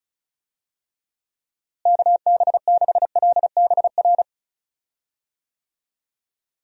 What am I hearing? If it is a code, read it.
Morse code, heard as a keyed tone, about 36 wpm, 700 Hz. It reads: KB6LBR
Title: USA Call Signs - 6 Characters - Repeated 2x - 35wpm